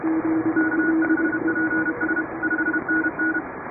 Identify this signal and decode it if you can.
Morse code, heard as a keyed tone, about 31 wpm, 1500 Hz. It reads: K5PS 5NN
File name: chatham / 2350-12CW